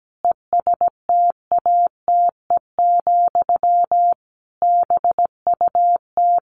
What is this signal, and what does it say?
Morse code, heard as a keyed tone, about 17 wpm, 700 Hz. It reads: ESTATE, BUT
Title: Sentences from Top 1200 Words 17wpm